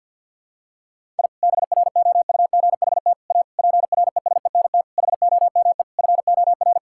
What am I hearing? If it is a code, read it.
Morse code, heard as a keyed tone, about 47 wpm, 700 Hz. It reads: I BROUGHT A PRESENT HOME FOR
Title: Sentences from Top 1000 Words - Review Part 5 - 50wpm